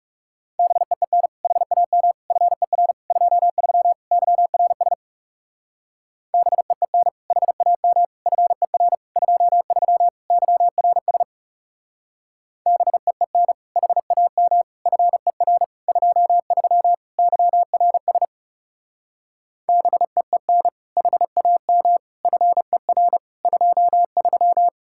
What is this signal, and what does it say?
Morse code, heard as a keyed tone, about 37 wpm, 700 Hz. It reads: BEEN HAM FER 23 YRS BEEN HAM FER 23 YRS BEEN HAM FER 23 YRS BEEN HAM FER 23
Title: Speed-Racing Common QSO Phrases Encore - 30wpm